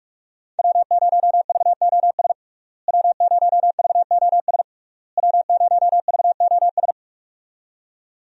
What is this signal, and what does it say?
Morse code, heard as a keyed tone, about 47 wpm, 700 Hz. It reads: W0VOS W0VOS W0VOS
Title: USA Call Signs - 5 Characters - Repeated 3x - 45wpm